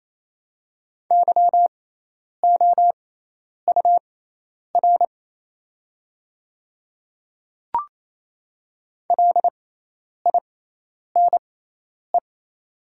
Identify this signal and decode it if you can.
Morse code, heard as a keyed tone, about 27 wpm, spaced as for 10 wpm, 700 Hz. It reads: YOUR LINE
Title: 6x Character Spacing - Top 200 Words - 28wpm